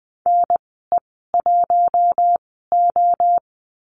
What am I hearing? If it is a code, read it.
Morse code, heard as a keyed tone, about 20 wpm, spaced as for 15 wpm, 700 Hz. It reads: NE1O
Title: 048 Lesson 9g - Number 4 - Call Signs 20wpm